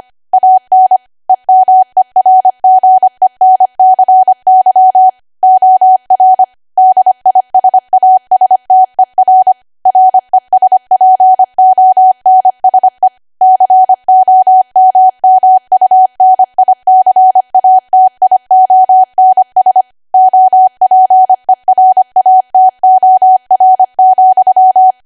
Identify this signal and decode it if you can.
Morse code, heard as a keyed tone, about 25 wpm, 750 Hz. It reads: AN EMERGENCY OR DISASTER RESPONSE COMMUNICATIONS OPERATOR,
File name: Archive / 25 WPM / 25WPM